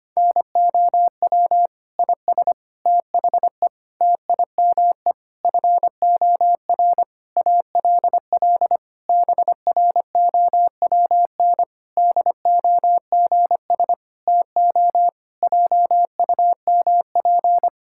Morse code, heard as a keyed tone, about 25 wpm, 700 Hz. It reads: NOW IS THE TIME FOR ALL BROWN DOGS TO JUMP
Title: Pangram Challenge v2 - 25wpm